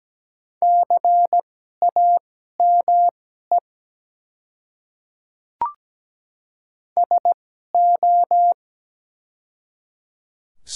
Morse code, heard as a keyed tone, about 17 wpm, spaced as for 12 wpm, 700 Hz. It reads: CAME SO
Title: 2x Character Spacing - Top 200 Words - 17wpm